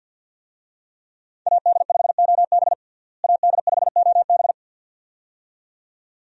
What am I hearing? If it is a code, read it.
Morse code, heard as a keyed tone, about 53 wpm, 700 Hz. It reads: AD5OB AD5OB
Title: USA Call Signs - 5 Characters - Repeated 2x - 50wpm